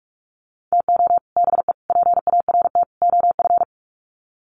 Morse code, heard as a keyed tone, about 44 wpm, 700 Hz. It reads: TO BE PART OF
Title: Sets of 4 Words 45wpm